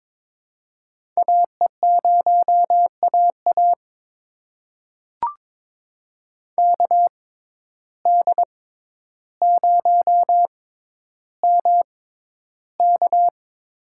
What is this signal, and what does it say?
Morse code, heard as a keyed tone, about 22 wpm, spaced as for 11 wpm, 700 Hz. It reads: AE0AA KD0MK
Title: USA Call Signs - 5 Characters - 6x Character Spacing - 22wpm